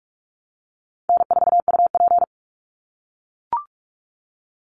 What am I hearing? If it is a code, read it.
Morse code, heard as a keyed tone, about 44 wpm, 700 Hz. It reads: N4UP
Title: Easy Call Signs - 45wpm